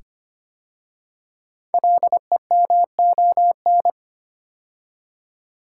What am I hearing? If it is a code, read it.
Morse code, heard as a keyed tone, about 25 wpm, 700 Hz. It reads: LEMON
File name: 5 Letter Words - First 3 Letters Given - 25wpm